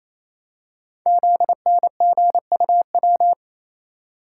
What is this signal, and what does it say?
Morse code, heard as a keyed tone, about 27 wpm, 700 Hz. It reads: ZNGUW
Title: Code Groups - 5 Letters - 28wpm